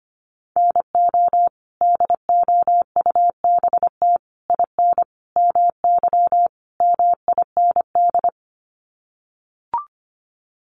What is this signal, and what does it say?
Morse code, heard as a keyed tone, about 25 wpm, 700 Hz. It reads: NO DOUBT IN MY MIND
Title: Sets of 5 Words Encore2 25wpm